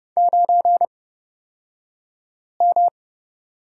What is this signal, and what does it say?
Morse code, heard as a keyed tone, about 30 wpm, spaced as for 5 wpm, 700 Hz. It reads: 9M
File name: Lesson 36d - BT - Two Characters 30wpm